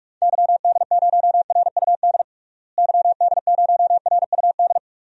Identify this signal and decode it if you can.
Morse code, heard as a keyed tone, about 46 wpm, 690 Hz. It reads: YD0RUD YD0RUD
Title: DXCC-100 Call Signs - 6 Characters - Repeated 2x - 45wpm